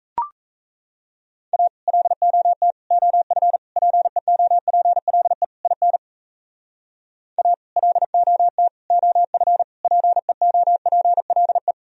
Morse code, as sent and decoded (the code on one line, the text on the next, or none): .- / .-.. --- - / --- ..-. / .--. --- .--. .-.. / .. -. / .- / .-.. --- - / --- ..-. / .--. --- .--. .-.. .
A LOT OF PEOPLE IN A LOT OF PEOPLE